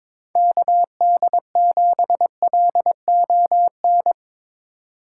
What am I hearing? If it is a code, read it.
Morse code, heard as a keyed tone, about 22 wpm, 700 Hz. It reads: KD7LON
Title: USA Call Signs - 6 Characters - 2x Character Spacing - 22wpm